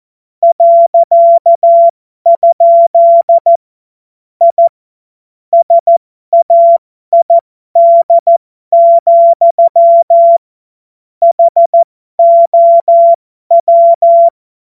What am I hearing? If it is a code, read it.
Morse code, heard as a keyed tone, about 14 wpm, spaced as for 12 wpm, 680 Hz. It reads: .? I SAID, HOW